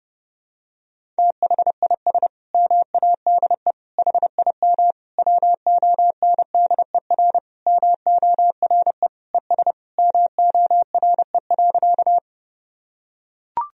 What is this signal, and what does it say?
Morse code, heard as a keyed tone, about 32 wpm, 700 Hz. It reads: THIS MADE HIM WONDER MORE ES MORE.
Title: Sentences from Top 500 Words and Abbreviations 30wpm